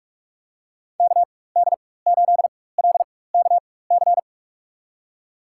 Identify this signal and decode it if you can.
Morse code, heard as a keyed tone, about 41 wpm, spaced as for 20 wpm, 700 Hz. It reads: KD8LKC
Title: USA Call Signs - 6 Characters - 4x Character Spacing - 45wpm